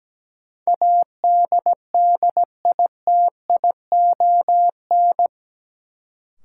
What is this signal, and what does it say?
Morse code, heard as a keyed tone, about 17 wpm, 700 Hz. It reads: ADDITION
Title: Sentences from Top 1500 Words - New Word Drill Alt - 17wpm